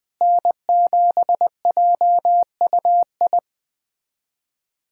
Morse code, heard as a keyed tone, about 21 wpm, 700 Hz. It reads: N7JUI
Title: USA Callsign DE Callsign - 5 Characters - 20wpm